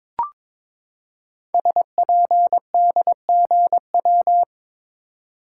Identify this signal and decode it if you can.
Morse code, heard as a keyed tone, about 22 wpm, 700 Hz. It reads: SPDGW
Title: Code Groups - 5 Letters - 22wpm